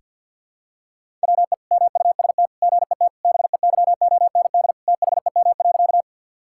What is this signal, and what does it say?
Morse code, heard as a keyed tone, about 50 wpm, 700 Hz. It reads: WE MUST GET BEYOND THEM.
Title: Sentences from Top 1000 Words - Review Part 6 - 50wpm